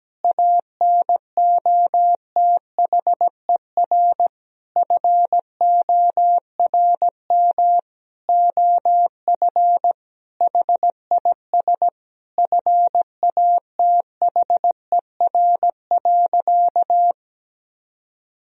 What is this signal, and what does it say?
Morse code, heard as a keyed tone, about 17 wpm, 700 Hz. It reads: ANOTHER FORM OF HIS FATHER.
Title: Sentences from Top 1000 Words - Review Part 1 - 17wpm